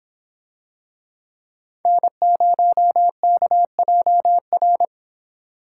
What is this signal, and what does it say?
Morse code, heard as a keyed tone, about 25 wpm, 700 Hz. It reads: N0KJR